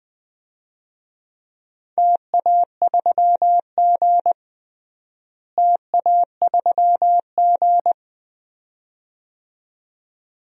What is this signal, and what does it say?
Morse code, heard as a keyed tone, about 20 wpm, 700 Hz. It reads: TA3G TA3G
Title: DXCC-100 Call Signs - 4 Characters - Repeated 2x - 20wpm